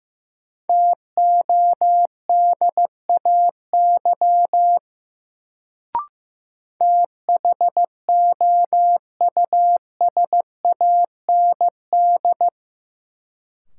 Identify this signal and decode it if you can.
Morse code, heard as a keyed tone, about 15 wpm, 700 Hz. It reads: TODAY THOUSAND
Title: Sentences from Top 600 Words - New Word Drill - 15wpm